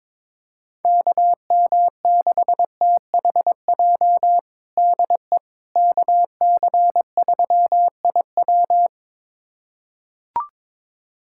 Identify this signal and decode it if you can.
Morse code, heard as a keyed tone, about 22 wpm, 700 Hz. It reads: KM6THJ DE KC3IW
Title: USA Callsign DE Callsign 22wpm